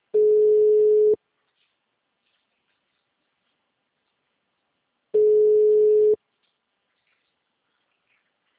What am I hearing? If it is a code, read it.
Ringback tone (Europe, 425 Hz).